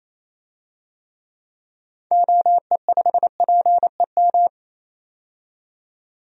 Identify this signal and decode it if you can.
Morse code, heard as a keyed tone, about 28 wpm, 700 Hz. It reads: OE5PEM
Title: DXCC-100 Call Signs - 6 Characters - 28wpm